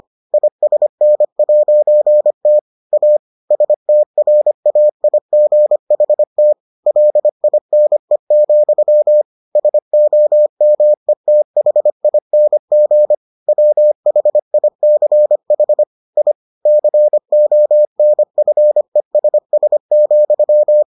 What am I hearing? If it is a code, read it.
Morse code, heard as a keyed tone, about 25 wpm, 600 Hz. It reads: ISN'T A STRAIGHT LINE, SOMETHING WHICH I CONFESS,